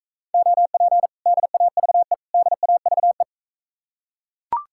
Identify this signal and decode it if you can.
Morse code, heard as a keyed tone, about 41 wpm, 700 Hz. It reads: OP DAVE DAVE